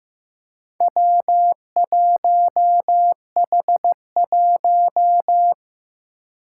Morse code, heard as a keyed tone, about 16 wpm, 700 Hz. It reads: W1H1
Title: Four Challenging Letter-Number Combinations 15wpm